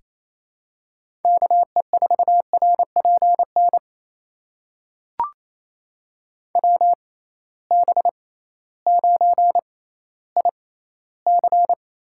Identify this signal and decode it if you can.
Morse code, heard as a keyed tone, about 28 wpm, spaced as for 15 wpm, 700 Hz. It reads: KE4RPN WB9IC
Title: USA Call Signs - 6 Characters - 6x Character Spacing - 28wpm